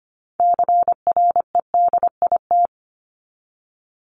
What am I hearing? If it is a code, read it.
Morse code, heard as a keyed tone, about 25 wpm, 700 Hz. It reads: CREDIT
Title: Top 751-1000 Words 25wpm